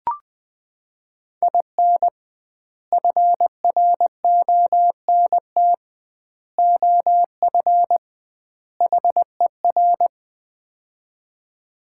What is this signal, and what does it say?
Morse code, heard as a keyed tone, about 20 wpm, 700 Hz. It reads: IN FRONT OF HER
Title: 2x Word-Spacing - Sets of 4 Words - 20wpm